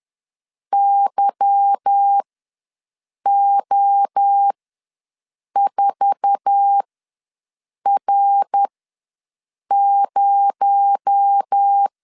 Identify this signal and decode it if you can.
Morse code, heard as a keyed tone, about 10 wpm, spaced as for 6 wpm, 800 Hz. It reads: YO4R0
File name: LEZIONI / 9 28 ws 10wpm 4124